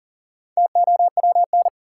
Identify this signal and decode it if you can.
Morse code, heard as a keyed tone, about 40 wpm, 700 Hz. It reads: TOWN
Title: Sentences from Top 500 Words - New Word Drill - 40wpm